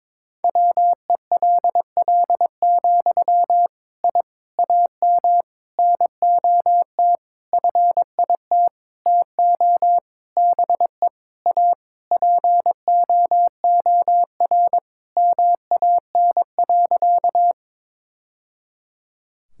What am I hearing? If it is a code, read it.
Morse code, heard as a keyed tone, about 22 wpm, 700 Hz. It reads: WELL, I AM NOT FIT TO BE A POOR MAN.
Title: Sentences from Top 1000 Words - Review Part 5 - 22wpm